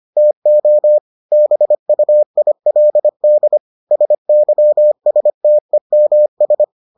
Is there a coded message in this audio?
Yes: Morse code from 0.2 to 6.6 s, heard as a keyed tone, about 25 wpm, 600 Hz. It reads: TO BUILD SYSTEMS